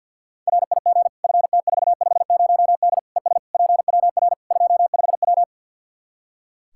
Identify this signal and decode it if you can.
Morse code, heard as a keyed tone, about 52 wpm, 700 Hz. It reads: RIG FT450D ES PWR 25W
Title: K1USN SST QSO Practice 50wpm